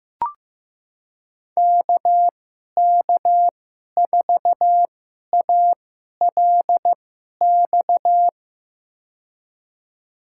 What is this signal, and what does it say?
Morse code, heard as a keyed tone, about 15 wpm, spaced as for 11 wpm, 700 Hz. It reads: KK4ALX
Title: USA Call Signs - 6 Characters - 2x Character Spacing - 15wpm